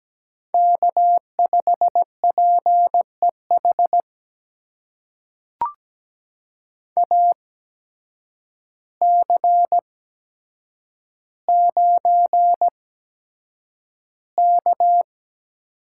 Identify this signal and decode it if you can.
Morse code, heard as a keyed tone, about 17 wpm, spaced as for 8 wpm, 700 Hz. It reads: K5PEH AC9K
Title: USA Call Signs - 8x Character Spacing - 17wpm